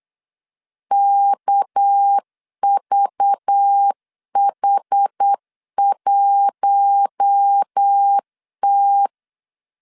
Morse code, heard as a keyed tone, about 8 wpm, 800 Hz. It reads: KVH1T
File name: LEZIONI / 6 PKX0 / 6 PK 8wpm 3003